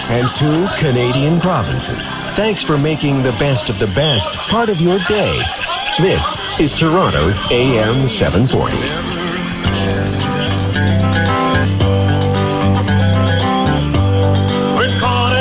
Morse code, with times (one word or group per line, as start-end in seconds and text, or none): none